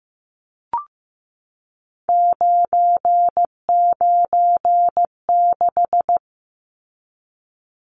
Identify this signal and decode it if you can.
Morse code, heard as a keyed tone, about 15 wpm, 700 Hz. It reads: 996
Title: Numbers - 3-Digit 15wpm